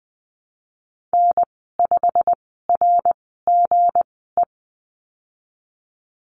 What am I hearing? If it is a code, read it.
Morse code, heard as a keyed tone, about 20 wpm, spaced as for 15 wpm, 700 Hz. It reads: N5RGE
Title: 161 Lesson 26g - Letter g - Call Signs 20wpm